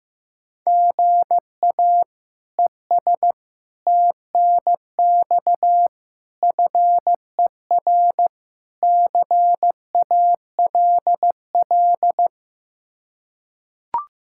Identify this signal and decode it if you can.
Morse code, heard as a keyed tone, about 15 wpm, 700 Hz. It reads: GA ES TNX FER CALL
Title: K1USN SST QSO Practice 15wpm